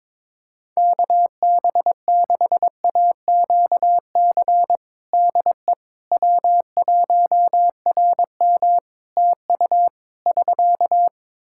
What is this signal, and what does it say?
Morse code, heard as a keyed tone, about 21 wpm, 700 Hz. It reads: KB6AQC DE W1RM TU <SK>
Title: Practice QSO - Exchange 4 - Ending - 22wpm